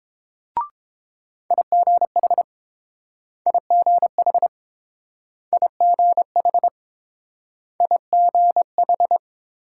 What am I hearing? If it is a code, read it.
Morse code, heard as a keyed tone, about 30 wpm, 700 Hz. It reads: IGH IGH IGH IGH